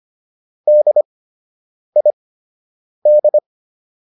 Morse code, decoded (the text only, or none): DID